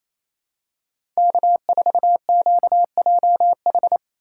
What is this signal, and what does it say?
Morse code, heard as a keyed tone, about 28 wpm, 700 Hz. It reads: K4QJH
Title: USA Call Signs - 2x Character Spacing - 28wpm